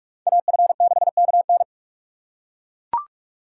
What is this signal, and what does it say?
Morse code, heard as a keyed tone, about 42 wpm, 700 Hz. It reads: AF6KN
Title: USA Call Signs - 5 Characters - 3x Character Spacing - 45wpm